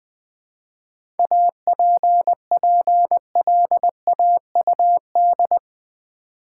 Morse code, heard as a keyed tone, about 20 wpm, 690 Hz. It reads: APPLAUD
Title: 7 Letter Words - First 3 Letters Given - 20wpm